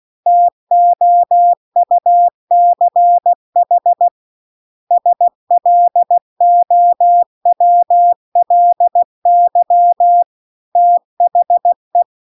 Morse code, heard as a keyed tone, about 16 wpm, 700 Hz. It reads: TOUCH SLOWLY THE